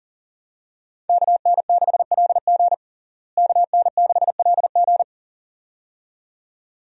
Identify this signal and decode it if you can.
Morse code, heard as a keyed tone, about 40 wpm, 700 Hz. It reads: KN6LG KN6LG
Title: USA Call Signs - 5 Characters - Repeated 2x - 40wpm